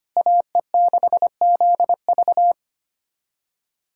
Morse code, heard as a keyed tone, about 25 wpm, 700 Hz. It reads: AE6ZV
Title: USA Callsign DE Callsign - 5 Characters - 25wpm